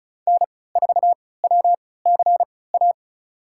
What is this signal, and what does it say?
Morse code, heard as a keyed tone, about 34 wpm, spaced as for 20 wpm, 700 Hz. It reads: N4WCA
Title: USA Call Signs - 5 Characters - 3x Character Spacing - 35wpm